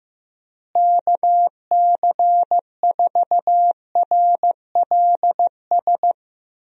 Morse code, heard as a keyed tone, about 15 wpm, 700 Hz. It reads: KC4RLS
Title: USA Call Signs - 6 Characters - Repeated 2x - 15wpm